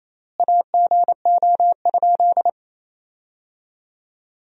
Morse code, heard as a keyed tone, about 27 wpm, 700 Hz. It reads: AGO?
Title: 4x Word-Spacing - Sentences from Top 500 Words - 28wpm